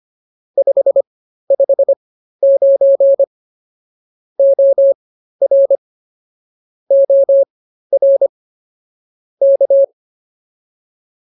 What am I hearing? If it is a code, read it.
Morse code, heard as a keyed tone, about 24 wpm, spaced as for 13 wpm, 550 Hz. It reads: 559 OR OR K